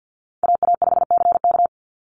Morse code, heard as a keyed tone, about 49 wpm, 700 Hz. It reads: AA5CK
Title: NA SOTA Chaser Callsigns 50wpm